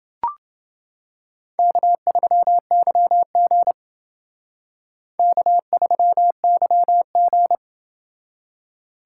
Morse code, heard as a keyed tone, about 29 wpm, 700 Hz. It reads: K3YG K3YG